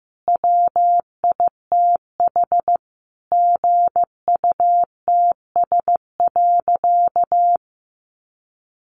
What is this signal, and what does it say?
Morse code, heard as a keyed tone, about 15 wpm, 700 Hz. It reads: WITH GUTS.
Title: A Sound Of Thunder 15wpm